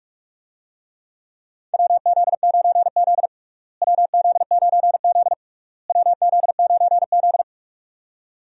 Morse code, heard as a keyed tone, about 45 wpm, 700 Hz. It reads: WZ9Z WZ9Z WZ9Z
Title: USA Call Signs - 4 Characters - Repeated 3x - 45wpm